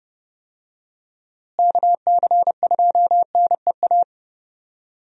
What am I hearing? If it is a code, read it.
Morse code, heard as a keyed tone, about 30 wpm, 700 Hz. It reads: KC2NEA
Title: USA Call Signs - 6 Characters - Repeated 2x - 30wpm